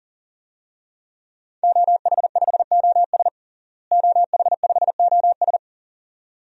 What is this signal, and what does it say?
Morse code, heard as a keyed tone, about 40 wpm, 700 Hz. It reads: OH5OS OH5OS
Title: DXCC-100 Call Signs - 5 Characters - Repeated 2x - 40wpm